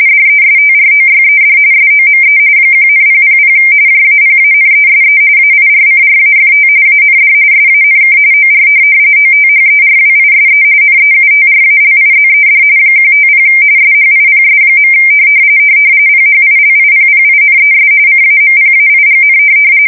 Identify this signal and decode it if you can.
RTTY, about 45.45 baud. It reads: DESTROYED SOME 1,000 JAPANESE AIRCRAFT AND SANK OR OTHERWISE NEUTRALIZED 150 SHIPS. PHILIPPINES CAMPAIGN
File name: fsk170 / USS-Massachusetts